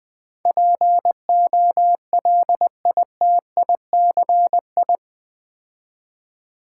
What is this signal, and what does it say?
Morse code, heard as a keyed tone, about 20 wpm, 700 Hz. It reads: POLITICI
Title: Build A Word - 4 Letters At A Time - 10-Letter Words - 20wpm